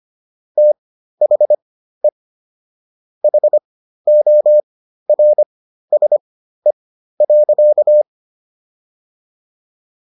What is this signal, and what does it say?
Morse code, heard as a keyed tone, about 25 wpm, spaced as for 13 wpm, 600 Hz. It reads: THE HORSE.